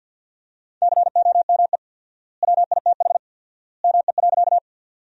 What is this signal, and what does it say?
Morse code, heard as a keyed tone, about 54 wpm, 700 Hz. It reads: COME WITH ME.
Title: 4x Word-Spacing - Sentences from Top 200 Words - 50wpm